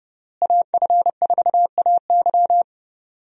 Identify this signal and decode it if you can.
Morse code, heard as a keyed tone, about 30 wpm, 700 Hz. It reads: AF4AY
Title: USA Call Signs - 5 Characters - Repeated 3x - 30wpm